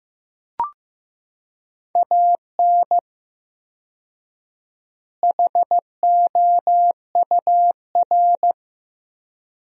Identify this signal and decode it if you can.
Morse code, heard as a keyed tone, about 15 wpm, 700 Hz. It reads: AN HOUR
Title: 4x Word-Spacing - Sentences from Top 400 Words - 15wpm